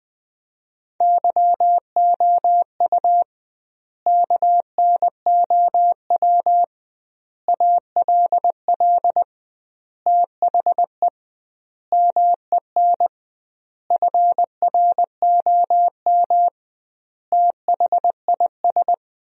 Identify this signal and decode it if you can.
Morse code, heard as a keyed tone, about 21 wpm, 700 Hz. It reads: YOU KNOW ALL THE MEN FROM THIS
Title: 2x Word-Spacing - Sentences from Top 300 Words - 20wpm